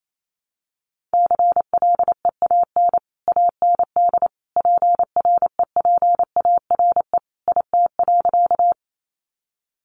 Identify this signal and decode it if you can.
Morse code, heard as a keyed tone, about 28 wpm, 700 Hz. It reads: CLEAN AND PREPARE IT.
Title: Sentences from Top 1100 Words 28wpm